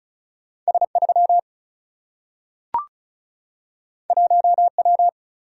Morse code, heard as a keyed tone, about 36 wpm, 700 Hz. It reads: S3 1W
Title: Single-Letter Single-Number 35wpm